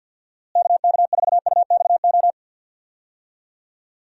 Morse code, heard as a keyed tone, about 48 wpm, 700 Hz. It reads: KK4UXO
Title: USA Call Signs - 6 Characters - Repeated 3x - 50wpm